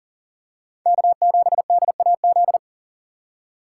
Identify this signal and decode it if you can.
Morse code, heard as a keyed tone, about 40 wpm, 700 Hz. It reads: K7DAZ
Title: USA Call Signs - 5 Characters - 6x Character Spacing - 40wpm